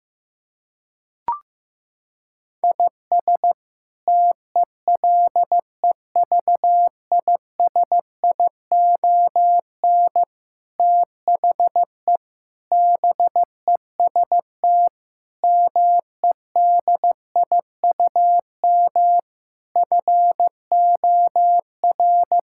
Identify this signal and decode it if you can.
Morse code, heard as a keyed tone, about 15 wpm, 700 Hz. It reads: IS TELEVISION THE BEST MEDIUM FOR
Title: Sentences from Top 1000 Words - Review Part 6 - 15wpm